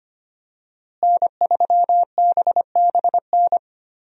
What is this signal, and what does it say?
Morse code, heard as a keyed tone, about 25 wpm, 700 Hz. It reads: N3BBN
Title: USA Call Signs - 5 Characters - Repeated 2x - 25wpm